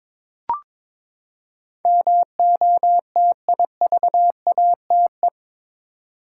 Morse code, heard as a keyed tone, about 22 wpm, 700 Hz. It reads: MOTIVATE
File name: practice / 22wpm / Eight Letter Words - 22wpm